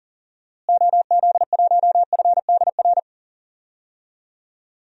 Morse code, heard as a keyed tone, about 40 wpm, 700 Hz. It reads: OZ1FDR
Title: DXCC-100 Call Signs - 6 Characters - Repeated 2x - 40wpm